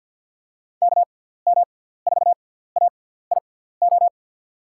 Morse code, heard as a keyed tone, about 52 wpm, spaced as for 17 wpm, 700 Hz. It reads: KM4AIO
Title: USA Call Signs - 6 Characters - 6x Character Spacing - 50wpm